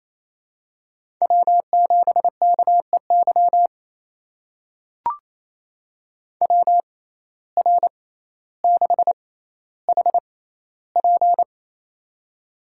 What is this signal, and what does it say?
Morse code, heard as a keyed tone, about 27 wpm, spaced as for 14 wpm, 700 Hz. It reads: W7KEY WR6HP
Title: USA Call Signs - 5 Characters - 6x Character Spacing - 28wpm